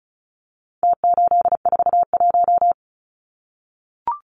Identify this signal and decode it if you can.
Morse code, heard as a keyed tone, about 34 wpm, 700 Hz. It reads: T841
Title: Numbers - Contest Style - 4-Digit 35wpm